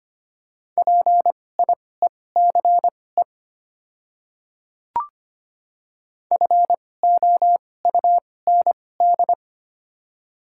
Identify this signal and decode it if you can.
Morse code, heard as a keyed tone, about 25 wpm, spaced as for 18 wpm, 700 Hz. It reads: PIECE FOUND
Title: 2x Character Spacing - Top 500 Words - 25wpm